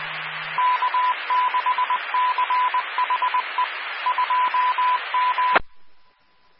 Morse code, heard as a keyed tone, about 20 wpm, 1000 Hz, with radio noise. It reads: K6CHE 2M
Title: Claremont, CA Police, Amateur Radio